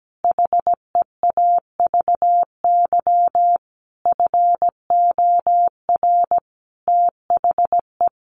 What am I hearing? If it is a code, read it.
Morse code, heard as a keyed tone, about 17 wpm, 700 Hz. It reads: HEAVY FOR THE
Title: Sentences from Top 1100 Words 17wpm